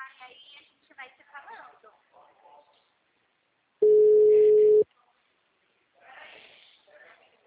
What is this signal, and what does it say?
Ringback tone (Europe, 425 Hz), with speech.
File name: out- -2008- - - .32845